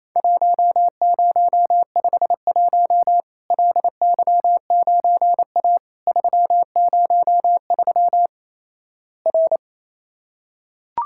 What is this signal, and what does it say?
Morse code, heard as a keyed tone, about 28 wpm, 700 Hz. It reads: 1051 LY9A 303 R
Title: Worked All Europe DX Contest - QTC Exchange - 28wpm